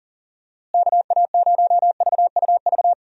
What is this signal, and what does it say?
Morse code, heard as a keyed tone, about 40 wpm, 700 Hz. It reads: KA0VUV
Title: USA Call Signs - 6 Characters - Repeated 2x - 40wpm